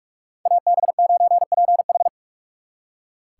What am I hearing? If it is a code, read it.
Morse code, heard as a keyed tone, about 59 wpm, 700 Hz. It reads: AB9PH